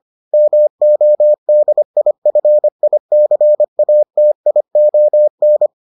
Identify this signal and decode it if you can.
Morse code, heard as a keyed tone, about 25 wpm, 600 Hz. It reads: MODIFICATION